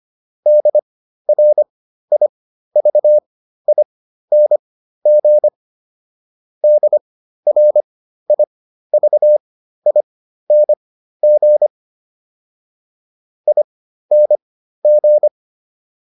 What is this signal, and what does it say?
Morse code, heard as a keyed tone, about 25 wpm, spaced as for 13 wpm, 600 Hz. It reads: DRIVING DRIVING ING